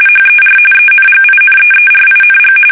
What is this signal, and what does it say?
RTTY, about 45.45 baud. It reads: HER ENGINEERING
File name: fsk850 / battleshipnj